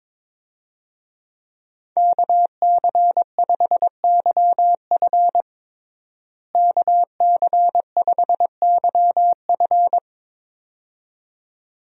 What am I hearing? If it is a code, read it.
Morse code, heard as a keyed tone, about 22 wpm, 700 Hz. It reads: KC5YF KC5YF
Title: USA Call Signs - 5 Characters - Repeated 2x - 22wpm